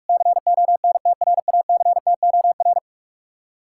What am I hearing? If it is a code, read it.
Morse code, heard as a keyed tone, about 56 wpm, 700 Hz. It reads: CONTRACTOR